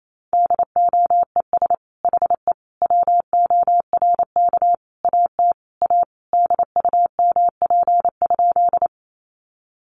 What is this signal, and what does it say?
Morse code, heard as a keyed tone, about 28 wpm, 700 Hz. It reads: DOES HE WORK AT A DUMP?
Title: Sentences from Top 1300 Words 28wpm